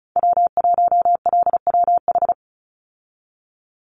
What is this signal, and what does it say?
Morse code, heard as a keyed tone, about 34 wpm, 700 Hz. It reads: W1LWH
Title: Call Signs Repeated 3x - 35wpm